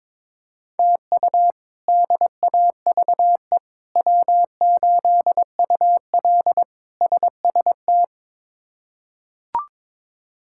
Morse code, heard as a keyed tone, about 22 wpm, 700 Hz. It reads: TU DAVE W8UL SST
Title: K1USN SST QSO Practice - 22wpm